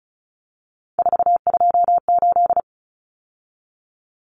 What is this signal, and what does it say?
Morse code, heard as a keyed tone, about 34 wpm, 700 Hz. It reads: 428
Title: Common Cut Numbers - 3-Digit 35wpm